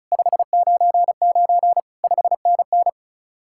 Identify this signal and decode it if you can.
Morse code, heard as a keyed tone, about 43 wpm, 700 Hz. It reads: 599 5NN
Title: Speed-Racing Common QSO Phrases Encore - 35wpm